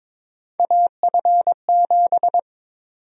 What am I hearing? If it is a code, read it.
Morse code, heard as a keyed tone, about 22 wpm, 700 Hz. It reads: AF7
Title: USA Call Sign Prefixes - Repeated 2x - 22wpm